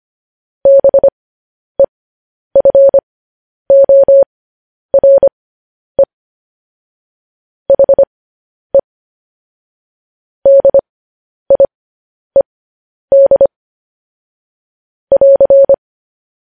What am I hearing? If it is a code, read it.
Morse code, heard as a keyed tone, about 25 wpm, spaced as for 10 wpm, 550 Hz. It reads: BEFORE HE DIED <AR>
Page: news-headlines-in-morse-code-at-10-wpm